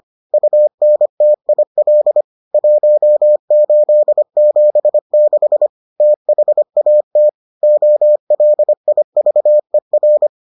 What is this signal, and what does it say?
Morse code, heard as a keyed tone, about 25 wpm, 600 Hz. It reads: UNTIL 1876 THAT OLIVER